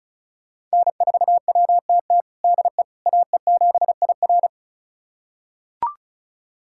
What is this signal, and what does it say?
Morse code, heard as a keyed tone, about 34 wpm, 700 Hz. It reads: N4WTT DE AE7IR
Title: USA Callsign DE Callsign - 5 Characters - 35wpm